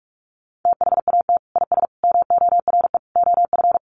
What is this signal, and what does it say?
Morse code, heard as a keyed tone, about 44 wpm, 700 Hz. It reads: THAT IS MORE OF
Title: Sentences from Top 1300 Words 45wpm